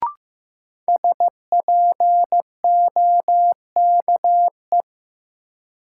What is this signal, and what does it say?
Morse code, heard as a keyed tone, about 15 wpm, 700 Hz. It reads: SPOKE
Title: Sentences from Top 1200 Words - New Word Drill Alt - 15wpm